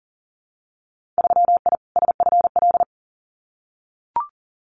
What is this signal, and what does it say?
Morse code, heard as a keyed tone, about 38 wpm, 700 Hz. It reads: 3I SFL
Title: Winter Field Day Exchanges 40wpm